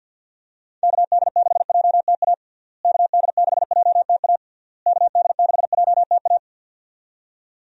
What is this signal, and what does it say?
Morse code, heard as a keyed tone, about 51 wpm, 700 Hz. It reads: KD6JTA KD6JTA KD6JTA
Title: USA Call Signs - 6 Characters - Repeated 3x - 50wpm